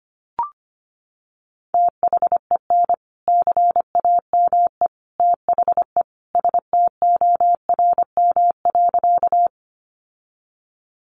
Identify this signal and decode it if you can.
Morse code, heard as a keyed tone, about 25 wpm, 700 Hz. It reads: THEN CAME THE STORM.
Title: Sentences from Top 1500 Words 25wpm